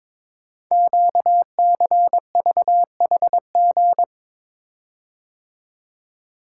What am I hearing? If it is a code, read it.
Morse code, heard as a keyed tone, about 22 wpm, 700 Hz. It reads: QCVHG
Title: Code Groups - 5 Letters - 22wpm